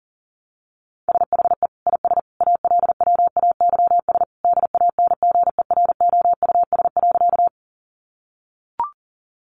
Morse code, heard as a keyed tone, about 40 wpm, 700 Hz. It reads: SHE IS ALWAYS DANGEROUS.
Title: Sentences from Top 1200 Words 40wpm